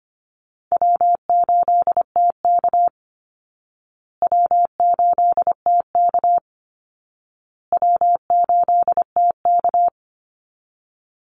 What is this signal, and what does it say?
Morse code, heard as a keyed tone, about 25 wpm, 700 Hz. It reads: W8TK W8TK W8TK
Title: Easy Call Signs Repeated 3x - 25wpm